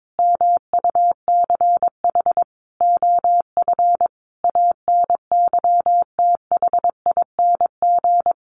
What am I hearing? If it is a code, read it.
Morse code, heard as a keyed tone, about 22 wpm, 700 Hz. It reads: MUCH OF ANYTHING
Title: Sentences from Top 1500 Words 22wpm